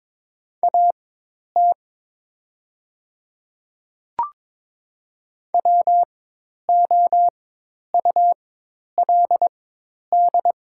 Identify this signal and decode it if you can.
Morse code, heard as a keyed tone, about 22 wpm, spaced as for 10 wpm, 700 Hz. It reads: AT WOULD
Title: 4x Character Spacing - Top 100 Words - 22wpm